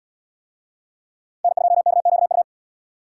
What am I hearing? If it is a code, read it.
Morse code, heard as a keyed tone, about 74 wpm, 700 Hz. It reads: N2KQU